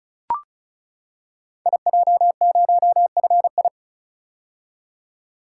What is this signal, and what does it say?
Morse code, heard as a keyed tone, about 36 wpm, 700 Hz. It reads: IJ0FI
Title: Code Groups - 5 Letter:Numbers - 35wpm